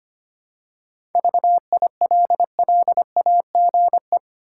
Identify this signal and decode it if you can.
Morse code, heard as a keyed tone, about 25 wpm, 700 Hz. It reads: VILLAGE
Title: Seven Letter Words - 25wpm